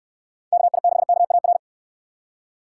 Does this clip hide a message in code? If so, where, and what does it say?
Yes: Morse code from 0.5 to 1.6 s, heard as a keyed tone, about 69 wpm, 700 Hz. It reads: KI6KAR